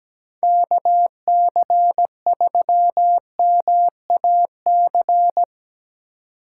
Morse code, heard as a keyed tone, about 17 wpm, 700 Hz. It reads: KC3MAC
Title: USA Call Signs - 6 Characters - 4x Character Spacing - 17wpm